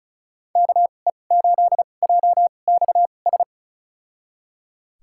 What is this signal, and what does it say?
Morse code, heard as a keyed tone, about 34 wpm, spaced as for 25 wpm, 700 Hz. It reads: KE8JXS
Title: USA Call Signs - 2x Character Spacing - 35wpm